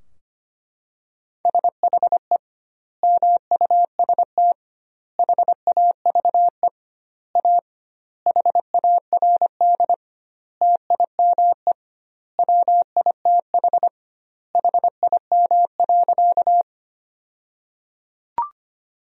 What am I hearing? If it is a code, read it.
Morse code, heard as a keyed tone, about 25 wpm, 700 Hz. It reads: SHE MUST HAVE A HARD TIME WITH HIM.
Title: 2x Word-Spacing - Sentences from Top 300 Words - 25wpm